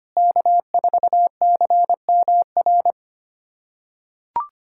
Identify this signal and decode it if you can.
Morse code, heard as a keyed tone, about 25 wpm, 700 Hz. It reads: K4CMR
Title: USA Callsign DE Callsign - 5 Characters - 25wpm